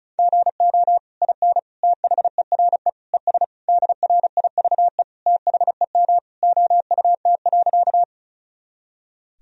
Morse code, heard as a keyed tone, about 36 wpm, 700 Hz. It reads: GO IN THERE ES DRIVE THEM OUT.
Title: Sentences from Top 500 Words and Abbreviations 35wpm